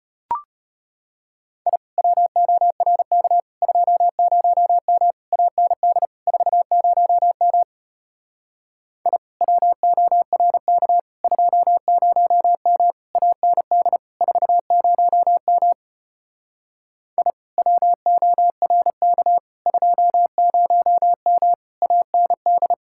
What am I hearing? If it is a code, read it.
Morse code, heard as a keyed tone, about 33 wpm, 700 Hz. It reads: I WORK 20M AND 40M I WORK 20M AND 40M I WORK 20M AND